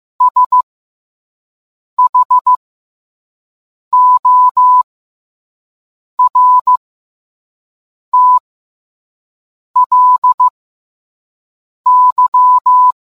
Morse code, heard as a keyed tone, about 15 wpm, spaced as for 5 wpm, 1000 Hz. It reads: SHORTLY